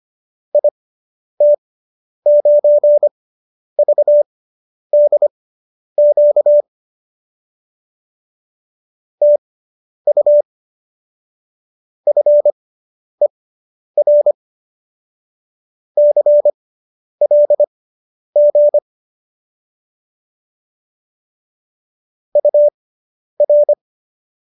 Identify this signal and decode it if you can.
Morse code, heard as a keyed tone, about 25 wpm, spaced as for 10 wpm, 590 Hz. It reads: IT9VDQ TU FER CLG UR